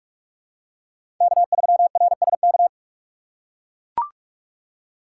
Morse code, heard as a keyed tone, about 41 wpm, 700 Hz. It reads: K3RSK
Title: USA Call Signs - 5 Characters - 6x Character Spacing - 45wpm